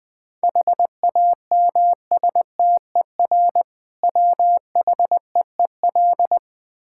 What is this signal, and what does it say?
Morse code, heard as a keyed tone, about 21 wpm, 690 Hz. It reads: HAMSTER WHEEL
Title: Difficult Words with Hints - 20wpm